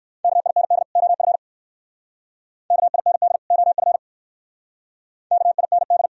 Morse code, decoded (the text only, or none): KIND OF KIND OF KIND